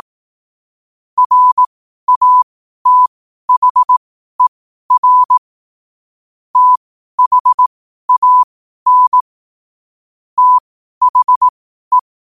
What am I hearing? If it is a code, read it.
Morse code, heard as a keyed tone, about 18 wpm, spaced as for 12 wpm, 1000 Hz. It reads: RATHER THAN THE
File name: qotdfiles / 12wpm